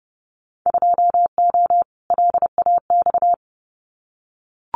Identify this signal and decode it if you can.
Morse code, heard as a keyed tone, about 30 wpm, 700 Hz. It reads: 2O LAX
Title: Winter Field Day Exchanges 30wpm